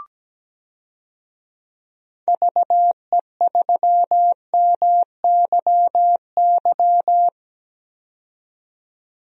Morse code, heard as a keyed tone, about 17 wpm, 700 Hz. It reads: VE3MYY
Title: DXCC-100 Call Signs - 6 Characters - 17wpm